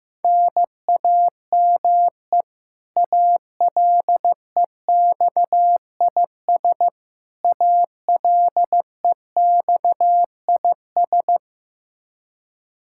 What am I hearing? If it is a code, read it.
Morse code, heard as a keyed tone, about 16 wpm, 700 Hz. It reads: NAME ALEXIS ALEXIS
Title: K1USN SST QSO Practice 15wpm